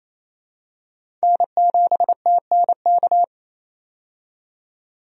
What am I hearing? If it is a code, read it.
Morse code, heard as a keyed tone, about 28 wpm, 700 Hz. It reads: N7TNK
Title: USA Call Signs - 5 Characters - 28wpm